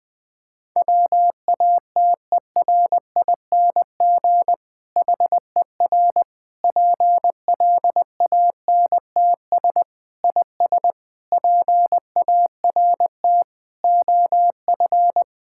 Morse code, heard as a keyed tone, about 20 wpm, 700 Hz. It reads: WATERING HER PLANTS IS PART OF
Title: Difficult Sentences with Hints - 20wpm